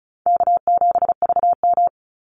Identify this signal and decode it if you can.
Morse code, heard as a keyed tone, about 36 wpm, 700 Hz. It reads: K7VM
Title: Call Signs - 35wpm